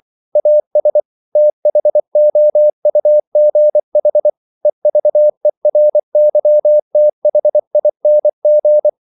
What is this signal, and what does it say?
Morse code, heard as a keyed tone, about 24 wpm, 600 Hz. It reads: AS THOUGH EVERYTHING
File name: CW / Practice / EnglishQuotations0424